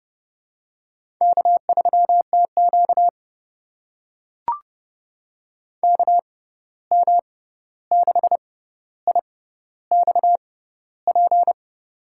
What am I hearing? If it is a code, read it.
Morse code, heard as a keyed tone, about 30 wpm, spaced as for 14 wpm, 700 Hz. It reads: K3TQ KM6IXP
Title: USA Call Signs - 6x Character Spacing - 30wpm